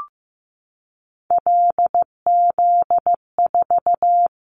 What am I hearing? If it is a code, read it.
Morse code, heard as a keyed tone, about 15 wpm, 700 Hz. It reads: LZ4
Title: Callsign Prefixes for Top 25 DXCC - 15wpm